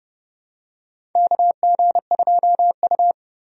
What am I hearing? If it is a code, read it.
Morse code, heard as a keyed tone, about 30 wpm, 700 Hz. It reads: KG2U
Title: USA Call Signs - 4 Characters - 30wpm